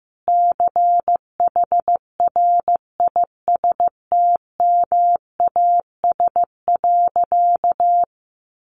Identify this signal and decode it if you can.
Morse code, heard as a keyed tone, about 15 wpm, 700 Hz. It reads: CHRISTMAS.
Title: Easy Phrases 15wpm